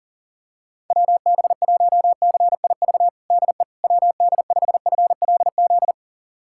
Morse code, heard as a keyed tone, about 40 wpm, 700 Hz. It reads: WB1CIV DE WD5FLZ
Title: USA Callsign DE Callsign - 6 Characters - 40wpm